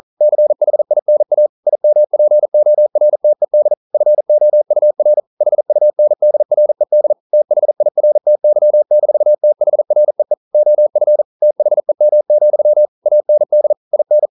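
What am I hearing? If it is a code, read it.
Morse code, heard as a keyed tone, about 42 wpm, 600 Hz. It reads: CHINA IMPORTED FOUR HUNDRED THIRTY-THREE OF THEM, AND IN
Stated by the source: Morse code practice at 41 wpm, from CW med Gnister.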